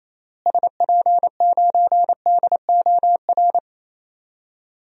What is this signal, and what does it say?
Morse code, heard as a keyed tone, about 28 wpm, 700 Hz. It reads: SP9DOR
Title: DXCC-100 Call Signs - 6 Characters - Repeated 3x - 28wpm